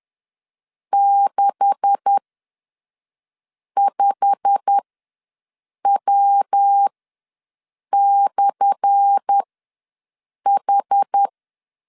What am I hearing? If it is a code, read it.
Morse code, heard as a keyed tone, about 10 wpm, spaced as for 5 wpm, 800 Hz. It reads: 65W/H
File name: LEZIONI / 10 SIGNS / 10A SLASH ws 10wpm 4121